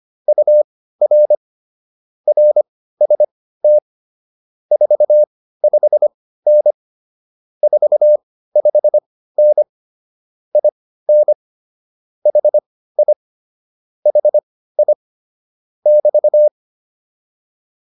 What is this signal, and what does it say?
Morse code, heard as a keyed tone, about 25 wpm, spaced as for 15 wpm, 600 Hz. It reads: UR RST 45N 45N IN HI HI <BT>